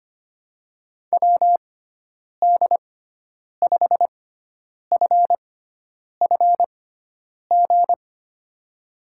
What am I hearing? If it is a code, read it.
Morse code, heard as a keyed tone, about 25 wpm, spaced as for 9 wpm, 700 Hz. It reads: WD5FFG
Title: USA Call Signs - 6 Characters - 6x Character Spacing - 25wpm